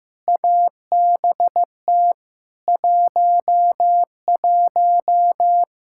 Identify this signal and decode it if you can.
Morse code, heard as a keyed tone, about 15 wpm, 700 Hz. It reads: ABT 11
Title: Practice QSO - Exchange 3 - More Info - 15wpm